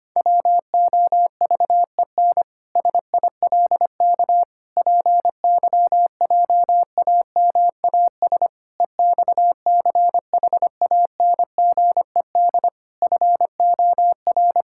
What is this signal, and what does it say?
Morse code, heard as a keyed tone, about 25 wpm, 700 Hz. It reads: WOVEN SILK PYJAMAS EXCHANGED FOR
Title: Pangram Challenge v2 - 25wpm